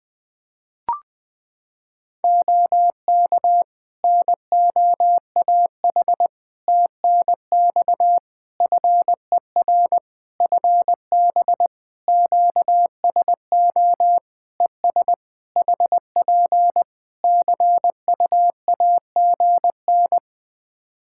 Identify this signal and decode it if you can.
Morse code, heard as a keyed tone, about 20 wpm, 700 Hz. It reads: OK NOAH TNX FER FB QSO ES HP CUAGN
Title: Practice QSO - Exchange 4 - Ending - 20wpm